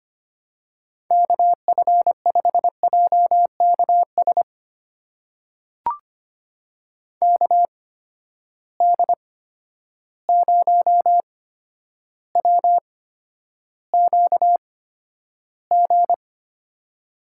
Morse code, heard as a keyed tone, about 25 wpm, spaced as for 11 wpm, 700 Hz. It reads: KF5JKS KD0WQG
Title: USA Call Signs - 6 Characters - 8x Character Spacing - 25wpm